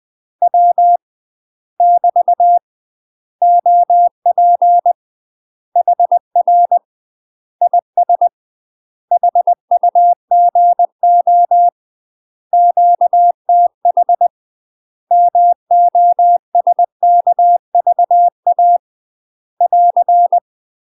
Morse code, heard as a keyed tone, about 20 wpm, 700 Hz. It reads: W <BT> OP HR IS HUGO QTH MOSKVA <AR>
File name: UK MASTER QSOs TRAINING / 20WPM / UK MASTER QSOs 0052